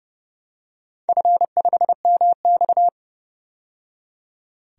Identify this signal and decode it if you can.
Morse code, heard as a keyed tone, about 30 wpm, 700 Hz. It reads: F5MX
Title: DXCC-100 Call Signs - 4 Characters - 30wpm